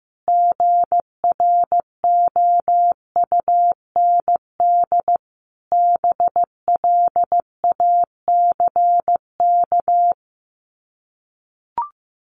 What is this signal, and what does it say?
Morse code, heard as a keyed tone, about 15 wpm, 700 Hz. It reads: GROUND BLACK
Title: Sets of 5 Words Encore 15wpm